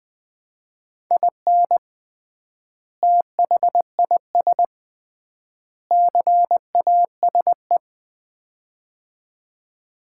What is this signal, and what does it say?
Morse code, heard as a keyed tone, about 21 wpm, 700 Hz. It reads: IN THIS CASE
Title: 3x Word-Spacing - Sets of 3 Words - 20wpm